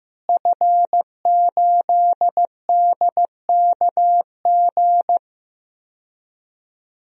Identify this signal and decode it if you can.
Morse code, heard as a keyed tone, about 15 wpm, 700 Hz. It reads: F8DKG
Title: DXCC-100 Call Signs - 5 Characters - Repeated 2x - 15wpm